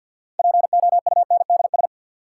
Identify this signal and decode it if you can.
Morse code, heard as a keyed tone, about 52 wpm, 700 Hz. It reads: POUNDS
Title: Sentences from Top 1000 Words - Review Part 2 - New Word Drill - 50wpm